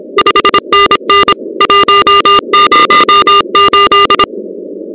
Morse code, heard as a keyed tone, about 25 wpm, 1210 Hz, with noise. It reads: 5NN 108
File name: contests / 2019-cw-wpx-kt5j-k5pi- -2330 0-16